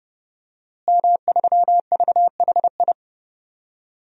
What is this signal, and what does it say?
Morse code, heard as a keyed tone, about 32 wpm, 700 Hz. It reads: M3VHI
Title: DXCC-100 Call Signs - 5 Characters - Repeated 2x - 30wpm